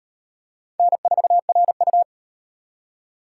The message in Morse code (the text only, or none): N4RU